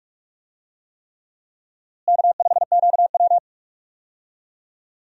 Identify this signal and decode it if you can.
Morse code, heard as a keyed tone, about 46 wpm, 700 Hz. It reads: K5QW
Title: USA Call Signs - 4 Characters - Repeated 3x - 45wpm